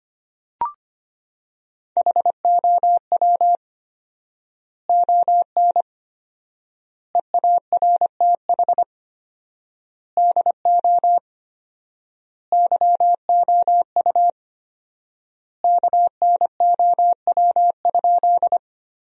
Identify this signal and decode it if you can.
Morse code, heard as a keyed tone, about 25 wpm, 700 Hz. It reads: HOW ON EARTH DO YOU KNOW?
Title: 4x Word-Spacing - Sentences from Top 200 Words - 25wpm